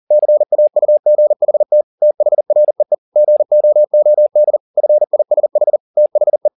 Morse code, heard as a keyed tone, about 40 wpm, 600 Hz. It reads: CAUGHT THREE GOOD FISH THE
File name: the old man and the sea 1 40wpm